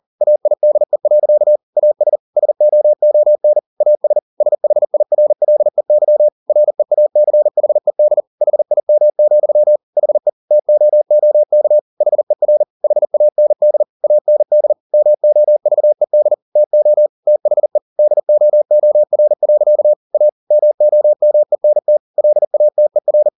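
Morse code, heard as a keyed tone, about 40 wpm, 600 Hz. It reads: AIDE. AS SOON AS SHIRLEY REACHED HIM, HE TOOK HER HAND AND MOVED TO THE DOOR. A MOMENT LATER
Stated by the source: Morse code at 40 wpm, practice text, CW med Gnister.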